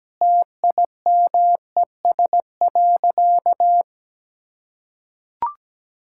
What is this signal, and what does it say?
Morse code, heard as a keyed tone, about 17 wpm, 700 Hz. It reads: TIMES.
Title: Sentences from Top 1000 Words - Review Part 6 - 17wpm